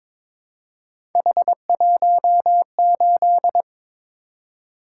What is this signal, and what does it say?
Morse code, heard as a keyed tone, about 22 wpm, 700 Hz. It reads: H18
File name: practice / 22wpm / Three Letter-Number Combinations - 22wpm